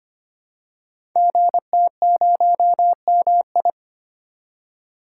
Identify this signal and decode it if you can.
Morse code, heard as a keyed tone, about 25 wpm, 700 Hz. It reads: GT0MI
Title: Code Groups - 5 Letter:Numbers - 25wpm